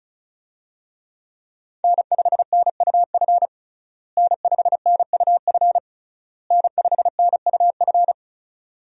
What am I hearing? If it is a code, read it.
Morse code, heard as a keyed tone, about 36 wpm, 700 Hz. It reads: N5NUF N5NUF N5NUF
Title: USA Call Signs - 5 Characters - Repeated 3x - 35wpm